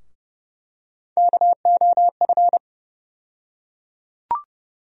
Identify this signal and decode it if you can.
Morse code, heard as a keyed tone, about 30 wpm, 700 Hz. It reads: KOF
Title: USA Call Sign Postfixes - 3 Letters 30wpm